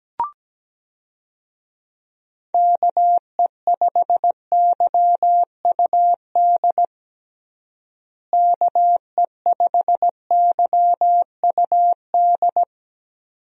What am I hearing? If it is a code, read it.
Morse code, heard as a keyed tone, about 17 wpm, 700 Hz. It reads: KE5YUD KE5YUD
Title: USA Call Signs - 6 Characters - Repeated 2x - 17wpm